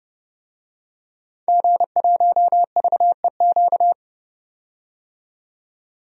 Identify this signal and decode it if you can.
Morse code, heard as a keyed tone, about 30 wpm, 700 Hz. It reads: G1VEQ
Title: DXCC-100 Call Signs - 5 Characters - 30wpm